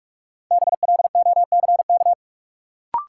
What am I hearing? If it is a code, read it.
Morse code, heard as a keyed tone, about 44 wpm, 700 Hz. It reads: BLOCK
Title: Sentences from Top 1600 Words - New Word Drill - 45wpm